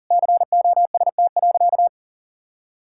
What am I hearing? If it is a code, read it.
Morse code, heard as a keyed tone, about 37 wpm, 700 Hz. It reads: COST.
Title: Sentences from Top 1000 Words - Review Part 5 - 40wpm